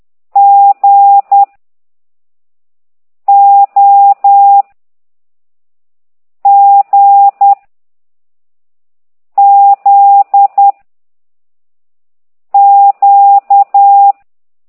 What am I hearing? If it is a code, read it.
Morse code, heard as a keyed tone, about 10 wpm, 800 Hz. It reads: G O G Z Q